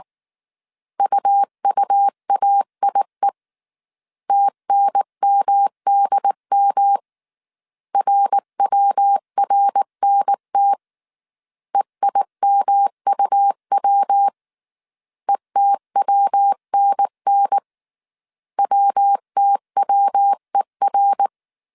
Morse code, heard as a keyed tone, about 19 wpm, 800 Hz. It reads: UUAIE TNMDM RWRNT EIMUW ETWNN WTWER
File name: CW / DET / 1 3 1500C S B 18wpm 1916